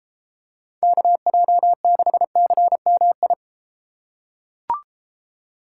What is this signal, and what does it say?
Morse code, heard as a keyed tone, about 32 wpm, 700 Hz. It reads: KJ6CMI